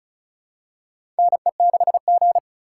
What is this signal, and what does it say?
Morse code, heard as a keyed tone, about 34 wpm, 700 Hz. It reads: NE6G
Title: USA Call Signs - 4 Characters - 3x Character Spacing - 35wpm